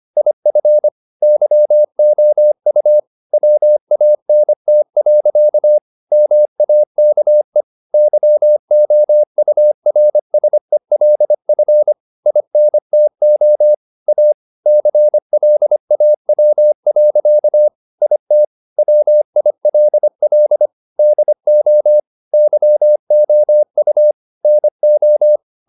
Morse code, heard as a keyed tone, about 25 wpm, 600 Hz. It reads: IF YOU WANT. MAKE YOURSELF INTO A CLAW. IT WILL DO YOU NO
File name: the old man and the sea part 1 25wpm